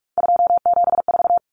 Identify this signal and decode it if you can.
Morse code, heard as a keyed tone, about 44 wpm, 700 Hz. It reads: 274
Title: Numbers - 3-Digit 45wpm